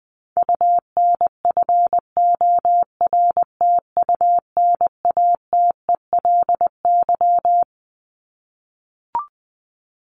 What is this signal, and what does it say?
Morse code, heard as a keyed tone, about 20 wpm, 700 Hz. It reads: UNFORTUNATELY
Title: Top 1751-2000 Words 20wpm